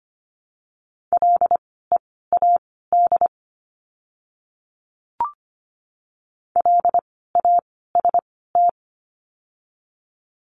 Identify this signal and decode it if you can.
Morse code, heard as a keyed tone, about 25 wpm, spaced as for 16 wpm, 700 Hz. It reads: LEAD LAST
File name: course / 25:10wpm / 094 Lesson 16f - Letter c - Words 25wpm